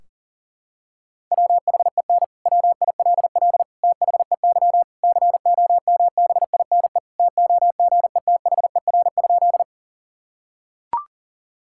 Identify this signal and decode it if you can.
Morse code, heard as a keyed tone, about 40 wpm, 700 Hz. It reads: WHEN WILL THEY COMBINE TOGETHER?
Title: Sentences from Top 1600 Words 40wpm